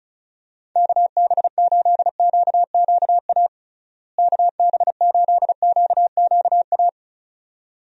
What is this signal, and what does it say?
Morse code, heard as a keyed tone, about 36 wpm, 700 Hz. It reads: KB8QQA KB8QQA
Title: USA Call Signs - 6 Characters - Repeated 2x - 35wpm